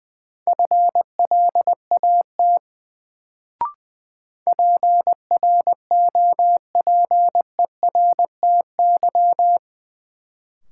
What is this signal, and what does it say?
Morse code, heard as a keyed tone, about 20 wpm, 700 Hz. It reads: FLAT PROPERTY
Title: Sentences from Top 1000 Words - New Word Drill - 20wpm